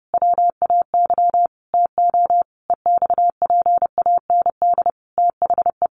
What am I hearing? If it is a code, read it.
Morse code, heard as a keyed tone, about 30 wpm, 700 Hz. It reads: WAY TO EXPAND THE
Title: Sentences from Top 1400 Words 30wpm